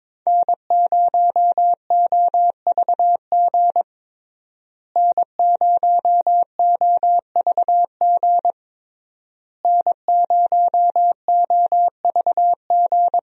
Morse code, heard as a keyed tone, about 21 wpm, 700 Hz. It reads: N0OVG N0OVG N0OVG
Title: USA Call Signs - 5 Characters - Repeated 3x - 22wpm